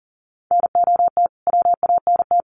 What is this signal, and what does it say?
Morse code, heard as a keyed tone, about 37 wpm, 700 Hz. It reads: NOT WANT
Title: Sets of 2 Words 40wpm